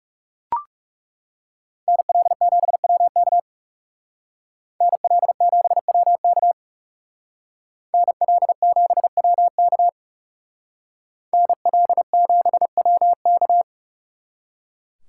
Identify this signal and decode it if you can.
Morse code, heard as a keyed tone, about 36 wpm, 700 Hz. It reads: NL7WK NL7WK NL7WK NL7WK
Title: Speed Racing - USA 2x2 Call Signs 30wpm